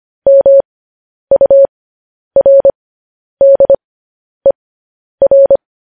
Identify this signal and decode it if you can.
Morse code, heard as a keyed tone, about 25 wpm, spaced as for 10 wpm, 550 Hz. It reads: MURDER
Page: News Headlines in Morse Code at 10 WPM